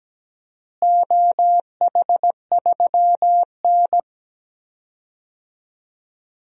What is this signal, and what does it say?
Morse code, heard as a keyed tone, about 17 wpm, 700 Hz. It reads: OH3N
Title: DXCC-100 Call Signs - 4 Characters - Repeated 3x - 17wpm